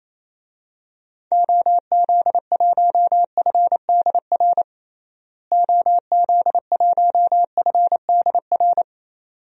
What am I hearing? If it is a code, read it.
Morse code, heard as a keyed tone, about 28 wpm, 700 Hz. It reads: OZ1FDR OZ1FDR
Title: DXCC-100 Call Signs - 6 Characters - Repeated 2x - 28wpm